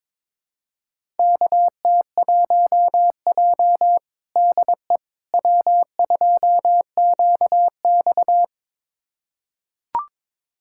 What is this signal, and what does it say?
Morse code, heard as a keyed tone, about 22 wpm, 700 Hz. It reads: KT1J DE W2QX
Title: USA Callsign DE Callsign - 4 Characters - 22wpm